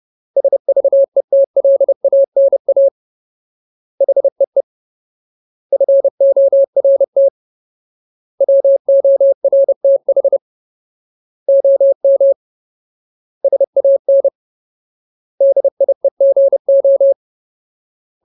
Morse code, heard as a keyed tone, about 30 wpm, 550 Hz. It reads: SVETLANA HEE FORT WORTH OM SAN DIEGO
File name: Ultimate Word List / 30WPM Ultimate